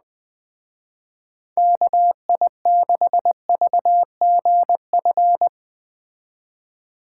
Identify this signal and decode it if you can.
Morse code, heard as a keyed tone, about 20 wpm, 700 Hz. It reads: KI6VGF
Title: Speed Racing - USA 2x3 Call Signs 17wpm